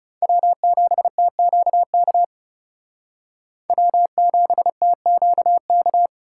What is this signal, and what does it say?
Morse code, heard as a keyed tone, about 34 wpm, 700 Hz. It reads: W7TQK W7TQK
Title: Speed Racing - USA 1x3 Call Signs 30wpm